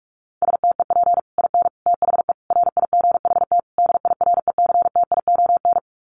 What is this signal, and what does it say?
Morse code, heard as a keyed tone, about 44 wpm, 700 Hz. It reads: STEP IN THE RIGHT DIRECTION
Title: Sets of 5 Words Encore 45wpm